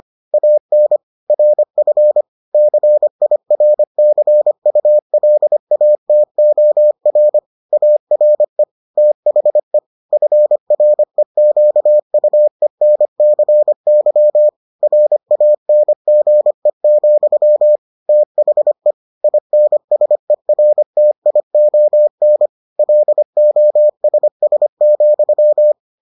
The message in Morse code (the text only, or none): AN RF CIRCULATOR ARE THE FREQUENCY RANGE, THE INSERTION LOSS,